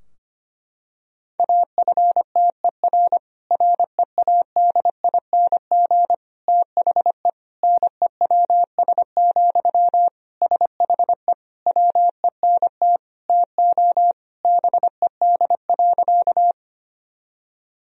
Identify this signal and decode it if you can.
Morse code, heard as a keyed tone, about 25 wpm, 700 Hz. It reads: AFTER READING THE NEWS, SHE WENT TO BED.
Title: Sentences from Top 1000 Words - Review Part 3 - 25wpm